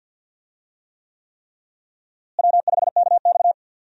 Morse code, heard as a keyed tone, about 48 wpm, 700 Hz. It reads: W5KX
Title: USA Call Signs - 4 Characters - Repeated 3x - 50wpm